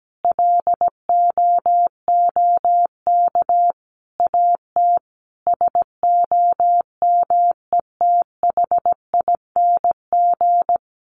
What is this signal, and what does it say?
Morse code, heard as a keyed tone, about 17 wpm, 700 Hz. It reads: LOOK AT SOMETHING
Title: Sentences from Top 800 Words 17wpm